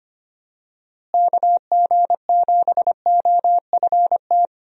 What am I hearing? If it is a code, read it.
Morse code, heard as a keyed tone, about 25 wpm, 700 Hz. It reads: KG7OFT
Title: USA Call Signs - 6 Characters - Repeated 3x - 25wpm